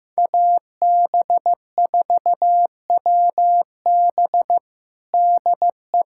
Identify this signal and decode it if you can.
Morse code, heard as a keyed tone, about 15 wpm, 700 Hz. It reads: AB4WB DE
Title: USA Callsign DE Callsign - 5 Characters - 15wpm